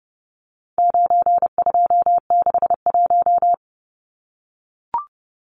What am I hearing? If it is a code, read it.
Morse code, heard as a keyed tone, about 30 wpm, 700 Hz. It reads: 9261
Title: Numbers - 4-Digit 30wpm